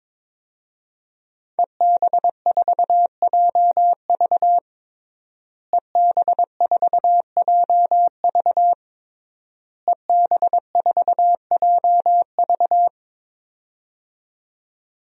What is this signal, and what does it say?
Morse code, heard as a keyed tone, about 23 wpm, 700 Hz. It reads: EB4JV EB4JV EB4JV
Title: DXCC-100 Call Signs - 5 Characters - Repeated 3x - 22wpm